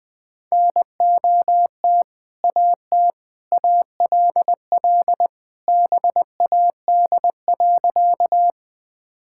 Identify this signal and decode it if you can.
Morse code, heard as a keyed tone, about 21 wpm, 700 Hz. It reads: NOT AT ALL BAD.
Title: Sentences with 2-Word Phrases - Spoken Beginning - 20wpm